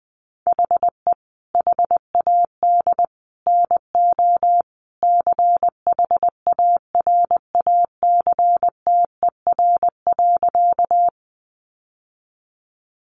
Morse code, heard as a keyed tone, about 20 wpm, 700 Hz. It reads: HE HAD NO CHARACTER.
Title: Sentences from Top 1200 Words 20wpm